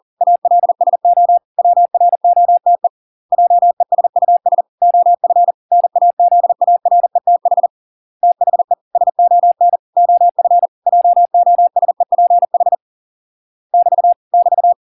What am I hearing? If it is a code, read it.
Morse code, heard as a keyed tone, about 40 wpm, 700 Hz. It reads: ALSO WROTE JESUS OF NAZARETH THE SON OF JOSEPH <BT> <BT>